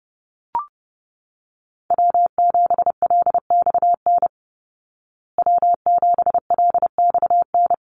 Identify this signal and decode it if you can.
Morse code, heard as a keyed tone, about 30 wpm, 700 Hz. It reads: W7LXN W7LXN
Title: Call Signs Repeated 3x - 30wpm